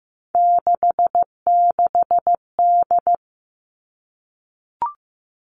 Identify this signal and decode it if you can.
Morse code, heard as a keyed tone, about 15 wpm, 700 Hz. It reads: 66D
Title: All Cut Numbers - 3-Digit 15wpm